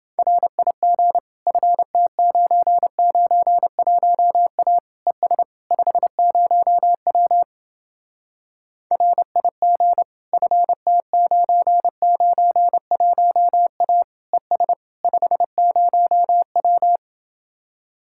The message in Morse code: RIG FT991A ES 50W RIG FT991A ES 50W

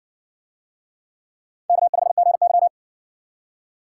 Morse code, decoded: K5GY

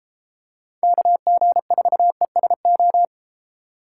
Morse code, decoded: KG4ESO

